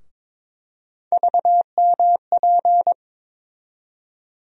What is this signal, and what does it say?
Morse code, heard as a keyed tone, about 22 wpm, 700 Hz. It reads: VMP
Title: USA Call Sign Postfixes - 3 Letters 22wpm